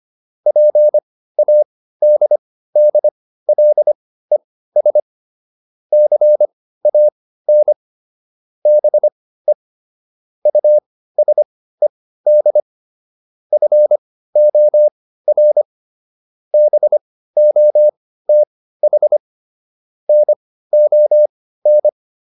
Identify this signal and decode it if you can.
Morse code, heard as a keyed tone, about 25 wpm, spaced as for 15 wpm, 600 Hz. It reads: PADDLES CAN BE USED FOR BOTH NON